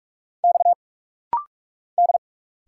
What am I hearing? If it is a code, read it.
Morse code, heard as a keyed tone, about 46 wpm, 700 Hz. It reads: X D
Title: Single Letters - Rapid-Fire 45wpm